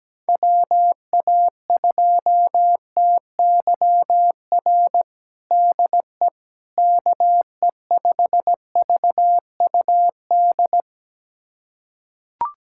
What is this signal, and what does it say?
Morse code, heard as a keyed tone, about 17 wpm, 700 Hz. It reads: WA2TYR DE KE5VUD
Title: USA Callsign DE Callsign - 6 Characters - 17wpm